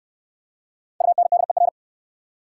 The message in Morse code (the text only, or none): UNDER